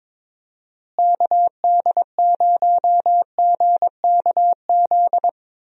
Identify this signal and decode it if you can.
Morse code, heard as a keyed tone, about 22 wpm, 700 Hz. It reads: KD0GKZ
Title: USA Call Signs - 6 Characters - Repeated 3x - 22wpm